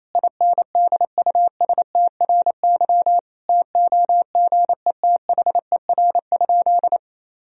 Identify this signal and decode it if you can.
Morse code, heard as a keyed tone, about 28 wpm, 700 Hz. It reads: INDUSTRY TOGETHER?
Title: Sentences from Top 1000 Words - Review Part 4 - 28wpm